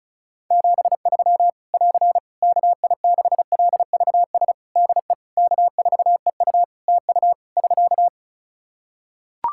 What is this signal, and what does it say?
Morse code, heard as a keyed tone, about 34 wpm, 700 Hz. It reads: 73 <AR> KI6LVS DE K4EU TU <SK>
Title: K1USN SST QSO Practice 35wpm